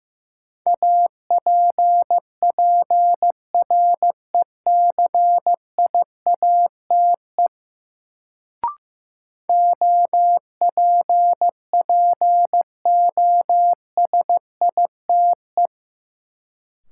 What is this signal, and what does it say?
Morse code, heard as a keyed tone, about 15 wpm, 700 Hz. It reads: APPRECIATE OPPOSITE
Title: Sentences from Top 1300 Words - New Word Drill - 15wpm